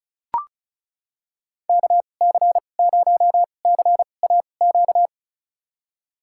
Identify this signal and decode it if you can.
Morse code, heard as a keyed tone, about 34 wpm, spaced as for 25 wpm, 700 Hz. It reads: KC0CAQ
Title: USA Call Signs - 6 Characters - 2x Character Spacing - 35wpm